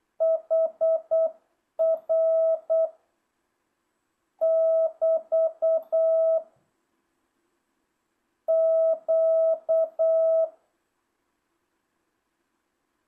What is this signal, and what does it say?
Morse code, heard as a keyed tone, about 8 wpm, 650 Hz. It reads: HR <BT> Q